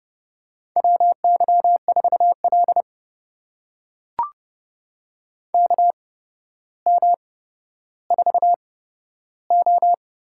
Morse code, heard as a keyed tone, about 30 wpm, spaced as for 13 wpm, 700 Hz. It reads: WY4L KM4O
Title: USA Call Signs - 8x Character Spacing - 30wpm